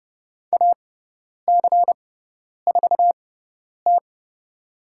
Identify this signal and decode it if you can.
Morse code, heard as a keyed tone, about 30 wpm, spaced as for 10 wpm, 700 Hz. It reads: AC4T
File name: Lesson 18g - Letter m - Call Signs 30wpm